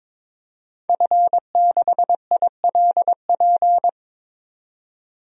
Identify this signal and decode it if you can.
Morse code, heard as a keyed tone, about 21 wpm, 700 Hz. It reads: F6ILP
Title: DXCC-100 Call Signs - 5 Characters - 22wpm